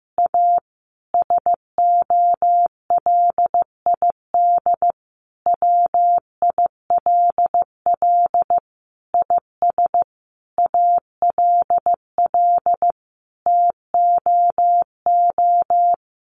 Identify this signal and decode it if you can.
Morse code, heard as a keyed tone, about 15 wpm, 700 Hz. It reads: A SOLID WILL IS ALL TOO
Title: Sentences from Top 1500 Words 15wpm